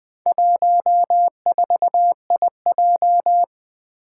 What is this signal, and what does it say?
Morse code, heard as a keyed tone, about 21 wpm, 700 Hz. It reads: 14IJ
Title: Four Challenging Letter-Number Combinations 20wpm